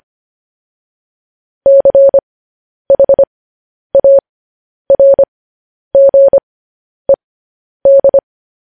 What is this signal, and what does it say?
Morse code, heard as a keyed tone, about 25 wpm, spaced as for 10 wpm, 550 Hz. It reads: CHARGED
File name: podcast / News headlines-1x-25x10